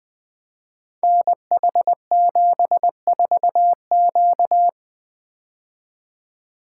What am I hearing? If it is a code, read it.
Morse code, heard as a keyed tone, about 20 wpm, 700 Hz. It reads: NH74Q
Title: Code Groups - 5 Letter:Numbers - 20wpm